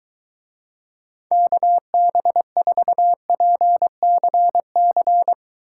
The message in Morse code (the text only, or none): KB4PCC